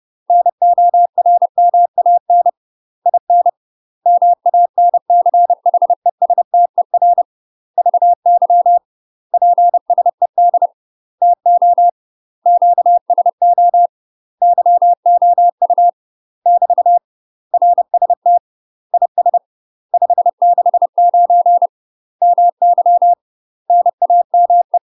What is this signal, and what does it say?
Morse code, heard as a keyed tone, about 30 wpm, 700 Hz. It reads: NORMAN IN MANCHESTER VY PSED TO QSO YOU <BT> RST IS 569 MY NAME
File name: UK MASTER QSOs TRAINING / 30WPM / UK MASTER QSOs 0029